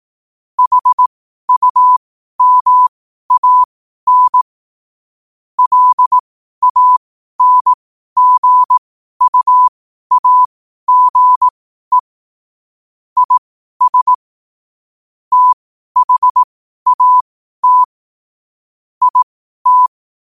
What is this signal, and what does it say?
Morse code, heard as a keyed tone, about 18 wpm, spaced as for 12 wpm, 1000 Hz. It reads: HUMAN LANGUAGE IS THAT IT